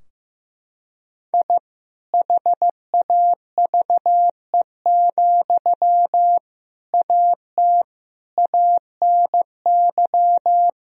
Morse code, heard as a keyed tone, about 15 wpm, 700 Hz. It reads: I HAVE, AT ANY
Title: Sentences from Top 1000 Words - Review Part 4 - 15wpm